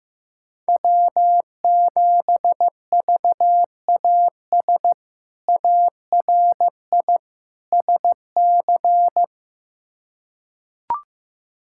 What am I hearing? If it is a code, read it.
Morse code, heard as a keyed tone, about 15 wpm, 700 Hz. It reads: W7VAS ARI SC
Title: K1USN SST QSO Practice - 15wpm